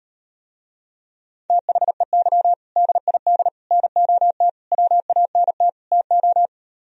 Morse code, heard as a keyed tone, about 39 wpm, 700 Hz. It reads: THEY DID NOT WANT TO